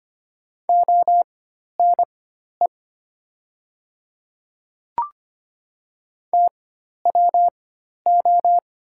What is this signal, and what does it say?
Morse code, heard as a keyed tone, about 25 wpm, spaced as for 12 wpm, 700 Hz. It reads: ONE TWO
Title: 4x Character Spacing - Top 100 Words - 25wpm